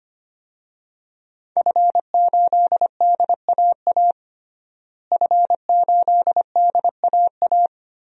Morse code, heard as a keyed tone, about 25 wpm, 700 Hz. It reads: F8DAA F8DAA
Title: DXCC-100 Call Signs - 5 Characters - Repeated 3x - 25wpm